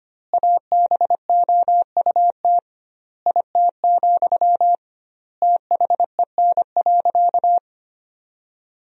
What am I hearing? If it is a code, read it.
Morse code, heard as a keyed tone, about 25 wpm, 700 Hz. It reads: ABOUT IT, THEN.
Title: 2x Word-Spacing - Sentences from Top 200 Words - 25wpm